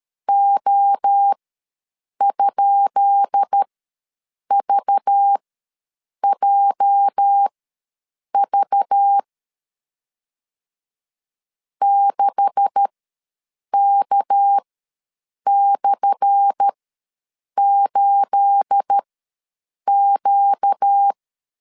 Morse code, heard as a keyed tone, about 13 wpm, spaced as for 7 wpm, 800 Hz. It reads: O?VJV 6K/8Q
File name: LEZIONI / 10 SIGNS / 10B INT ws 12wpm 3527